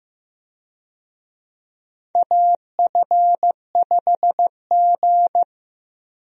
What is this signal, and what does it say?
Morse code, heard as a keyed tone, about 15 wpm, 700 Hz. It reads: AF5G
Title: USA Call Signs - 4 Characters - Repeated 2x - 15wpm